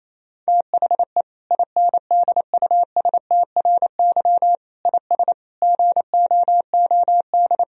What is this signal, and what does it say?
Morse code, heard as a keyed tone, about 28 wpm, 700 Hz. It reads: THE INDUSTRY IS GOOD
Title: Sentences from Top 1000 Words - Review Part 4 - 28wpm